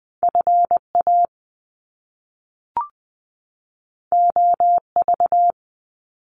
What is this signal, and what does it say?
Morse code, heard as a keyed tone, about 20 wpm, 700 Hz. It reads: FA OV
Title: Top Two Letter Combinations in Words 20wpm